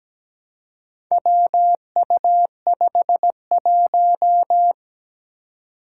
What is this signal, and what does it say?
Morse code, heard as a keyed tone, about 17 wpm, 700 Hz. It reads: WU51
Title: Four Challenging Letter-Number Combinations 17wpm